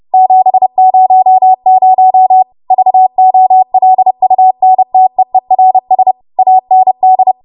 Morse code, heard as a keyed tone, about 30 wpm, 750 Hz. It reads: 700 VOLUNTEERS AND
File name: Morse / 30WPM